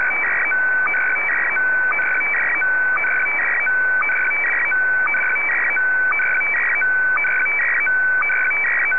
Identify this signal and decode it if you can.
SSTV (slow-scan television), Scottie DX.
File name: sstv / SDX-VK3AOB